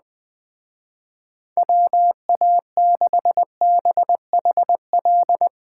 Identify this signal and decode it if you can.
Morse code, heard as a keyed tone, about 21 wpm, 700 Hz. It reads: WA6BHL